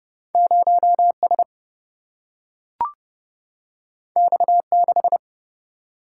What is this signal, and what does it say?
Morse code, heard as a keyed tone, about 31 wpm, 700 Hz. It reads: 0S X6
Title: Single-Letter Single-Number 30wpm